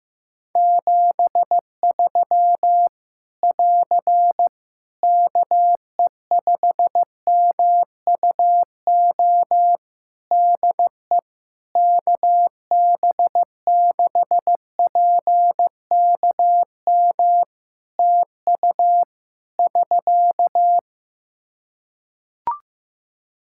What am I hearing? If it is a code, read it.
Morse code, heard as a keyed tone, about 15 wpm, 700 Hz. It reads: 73 <AR> KE5MUO DE KB6PKM TU <SK>
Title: Practice QSO - Exchange 4 - Ending - 15wpm